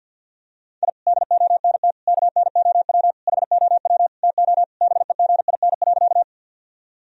Morse code, heard as a keyed tone, about 50 wpm, 700 Hz. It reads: I DONT KNOW HOW TO BEGIN.
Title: Sentences from Top 1000 Words - Review Part 1 - 50wpm